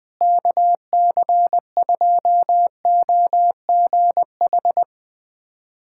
Morse code, heard as a keyed tone, about 20 wpm, 700 Hz. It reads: KC2OGH
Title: USA Callsign DE Callsign 20wpm